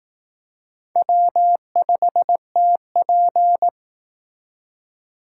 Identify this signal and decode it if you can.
Morse code, heard as a keyed tone, about 19 wpm, 700 Hz. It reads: W5TP